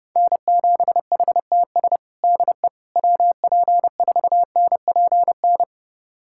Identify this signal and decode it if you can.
Morse code, heard as a keyed tone, about 30 wpm, 700 Hz. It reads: N7HTS DE WP4NPN
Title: USA Callsign DE Callsign 30wpm